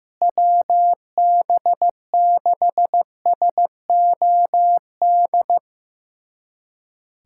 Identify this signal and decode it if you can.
Morse code, heard as a keyed tone, about 15 wpm, 700 Hz. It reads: WB6SOD
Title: USA Callsign DE Callsign - 6 Characters - 15wpm